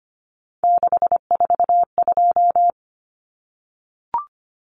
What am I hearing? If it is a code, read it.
Morse code, heard as a keyed tone, about 25 wpm, 690 Hz. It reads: 642
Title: Numbers - 3-Digit 25wpm